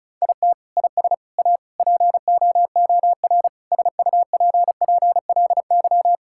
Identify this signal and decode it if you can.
Morse code, heard as a keyed tone, about 36 wpm, 700 Hz. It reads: IT IS A POOR SUPPLY
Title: Sentences from Top 1000 Words - Review Part 5 - 35wpm